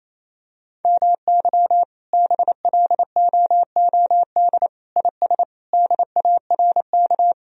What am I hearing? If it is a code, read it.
Morse code, heard as a keyed tone, about 27 wpm, 700 Hz. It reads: MY BLOOD IS DARK
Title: Sentences from Top 1000 Words - Review Part 2 - 28wpm